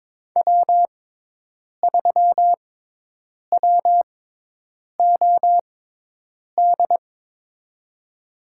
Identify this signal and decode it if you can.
Morse code, heard as a keyed tone, about 22 wpm, spaced as for 8 wpm, 700 Hz. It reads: W3WOD
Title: USA Call Signs - 5 Characters - 6x Character Spacing - 22wpm